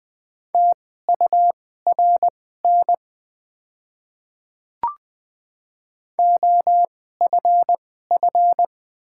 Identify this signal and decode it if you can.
Morse code, heard as a keyed tone, about 20 wpm, spaced as for 14 wpm, 700 Hz. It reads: TURN OFF
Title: 2x Character Spacing - Top 200 Words - 20wpm